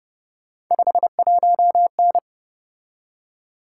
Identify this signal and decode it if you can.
Morse code, heard as a keyed tone, about 30 wpm, 700 Hz. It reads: 51N
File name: Speed-Racing - Three Letter-Number Combinations - 25wpm